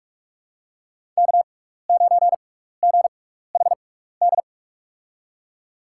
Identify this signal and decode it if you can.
Morse code, heard as a keyed tone, about 45 wpm, spaced as for 15 wpm, 700 Hz. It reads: K9GHD
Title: USA Call Signs - 5 Characters - 6x Character Spacing - 45wpm